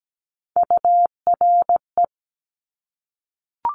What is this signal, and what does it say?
Morse code, heard as a keyed tone, about 17 wpm, 700 Hz. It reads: URE
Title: Top Three Letter Combinations in Words 17wpm